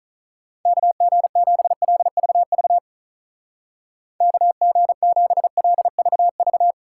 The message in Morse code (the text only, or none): KG7LVV KG7LVV